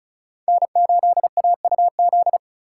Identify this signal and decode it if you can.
Morse code, heard as a keyed tone, about 36 wpm, 700 Hz. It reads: N8AUZ
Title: USA Call Signs - 6x Character Spacing - 35wpm